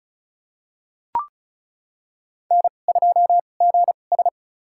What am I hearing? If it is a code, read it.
Morse code, heard as a keyed tone, about 36 wpm, spaced as for 25 wpm, 700 Hz. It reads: N2GS
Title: USA Call Signs - 2x Character Spacing - 35wpm